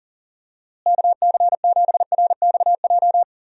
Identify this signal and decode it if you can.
Morse code, heard as a keyed tone, about 40 wpm, 700 Hz. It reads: KC7RXJ
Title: USA Call Signs - 6 Characters - Repeated 3x - 40wpm